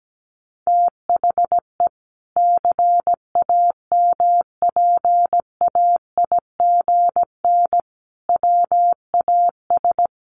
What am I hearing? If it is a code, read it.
Morse code, heard as a keyed tone, about 17 wpm, 700 Hz. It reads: THE CAMPAIGN WAS
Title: Sentences from Top 1400 Words 17wpm